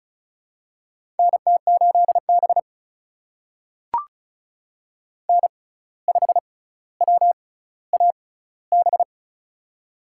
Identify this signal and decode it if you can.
Morse code, heard as a keyed tone, about 36 wpm, spaced as for 17 wpm, 700 Hz. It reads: NT8B N5WAB
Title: USA Call Signs - 6x Character Spacing - 35wpm